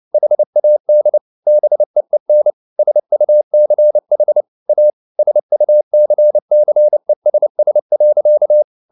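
Morse code, heard as a keyed tone, about 29 wpm, 600 Hz. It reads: HAD BEEN SUCH A SUCCESS.
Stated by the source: Morse code at 29 wpm, sending Ken Follett's novel Never in English, from CW med Gnister.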